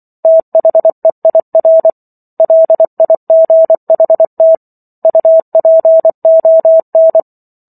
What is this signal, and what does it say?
Morse code, heard as a keyed tone, about 24 wpm, 650 Hz. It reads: THEIR LIGHT UPON